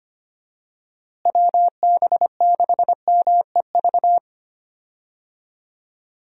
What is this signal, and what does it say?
Morse code, heard as a keyed tone, about 25 wpm, 700 Hz. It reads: WB6MEV
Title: USA Call Signs - 6 Characters - Repeated 3x - 25wpm